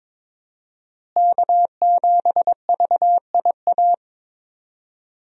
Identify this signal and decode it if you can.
Morse code, heard as a keyed tone, about 21 wpm, 700 Hz. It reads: K7VIA
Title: USA Call Signs - 5 Characters - Repeated 2x - 22wpm